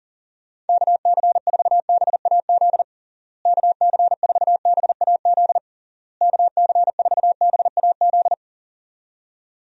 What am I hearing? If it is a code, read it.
Morse code, heard as a keyed tone, about 40 wpm, 700 Hz. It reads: KC4BAZ KC4BAZ KC4BAZ
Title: USA Call Signs - 6 Characters - Repeated 3x - 40wpm